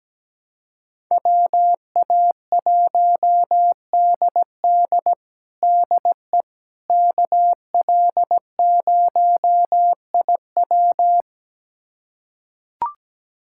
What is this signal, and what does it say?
Morse code, heard as a keyed tone, about 17 wpm, 700 Hz. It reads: WA1DD DE KL0IW
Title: USA Callsign DE Callsign - 5 Characters - 17wpm